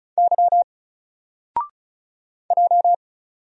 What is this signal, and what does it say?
Morse code, heard as a keyed tone, about 34 wpm, 700 Hz. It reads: Y J
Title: Single Letter-Number Alt 35wpm